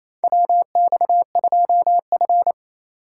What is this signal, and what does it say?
Morse code, heard as a keyed tone, about 28 wpm, 700 Hz. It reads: WX2F
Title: USA Call Signs - 4 Characters - 3x Character Spacing - 28wpm